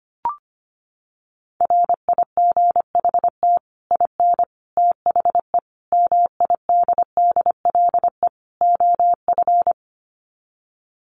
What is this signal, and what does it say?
Morse code, heard as a keyed tone, about 25 wpm, 700 Hz. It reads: RIGHT IN THE MIDDLE OF
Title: Sets of 5 Words 25wpm